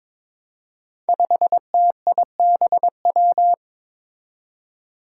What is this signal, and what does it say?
Morse code, heard as a keyed tone, about 22 wpm, 700 Hz. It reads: 5TIBW
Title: Five Challenging Letter-Number Combinations 22wpm